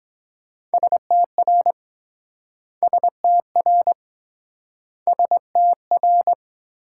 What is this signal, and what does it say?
Morse code, heard as a keyed tone, about 23 wpm, 700 Hz. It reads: STR STR STR